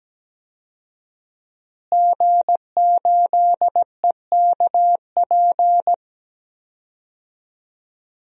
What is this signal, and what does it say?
Morse code, heard as a keyed tone, about 17 wpm, 700 Hz. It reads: G8EKP
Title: DXCC-100 Call Signs - 5 Characters - Repeated 3x - 17wpm